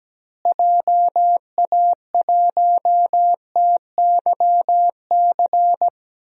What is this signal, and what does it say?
Morse code, heard as a keyed tone, about 17 wpm, 700 Hz. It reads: JA1TYC
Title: DXCC-100 Call Signs - 6 Characters - Repeated 2x - 17wpm